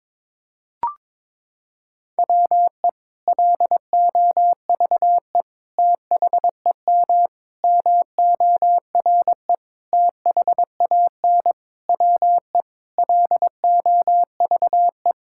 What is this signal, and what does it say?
Morse code, heard as a keyed tone, about 22 wpm, 700 Hz. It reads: WE LOVE THEM MORE THAN WE LOVE
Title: Sentences from Top 1600 Words 22wpm